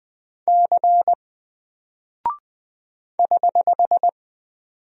Morse code, heard as a keyed tone, about 20 wpm, 700 Hz. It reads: C <HH>
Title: Single Character (Letter:Number:Punctuation:Prosign) Alt 20wpm